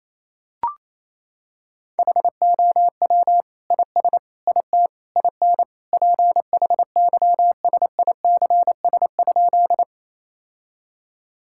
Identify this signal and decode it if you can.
Morse code, heard as a keyed tone, about 28 wpm, 700 Hz. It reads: HOW IS IT IN PHYSICS?
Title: Sentences from Top 1000 Words - Review Part 5 - 28wpm